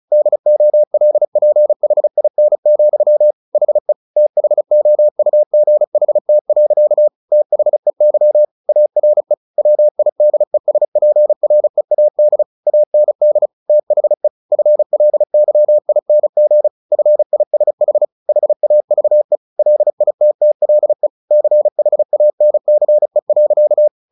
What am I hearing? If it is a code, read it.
Morse code, heard as a keyed tone, about 35 wpm, 600 Hz. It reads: DOLPHIN, HE THOUGHT. THEY ARE WIDESPREAD AND THE FLYING FISH HAVE LITTLE CHANCE.
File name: the old man and the sea / the old man and the sea 1 35wpm